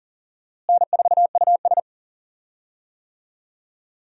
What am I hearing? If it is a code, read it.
Morse code, heard as a keyed tone, about 40 wpm, 700 Hz. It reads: N4US
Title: USA Call Signs - 4 Characters - Repeated 3x - 40wpm